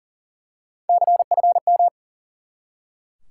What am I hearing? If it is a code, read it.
Morse code, heard as a keyed tone, about 40 wpm, 700 Hz. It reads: CFM
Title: Top 90 QSO Elements 40wpm